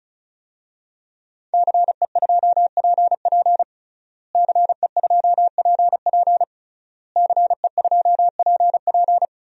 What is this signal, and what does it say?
Morse code, heard as a keyed tone, about 36 wpm, 700 Hz. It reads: CE2PP CE2PP CE2PP
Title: DXCC-100 Call Signs - 5 Characters - Repeated 3x - 35wpm